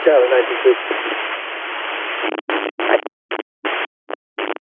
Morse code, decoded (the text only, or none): none